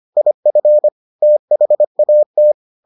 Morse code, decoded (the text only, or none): IF THAT